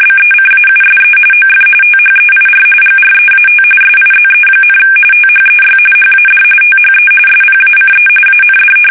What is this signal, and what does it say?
RTTY, about 45.45 baud. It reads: BECAUSE OF THEIR GREAT SIZE, THEY HAD THE SPACE AND
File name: fsk850 / battleshipnj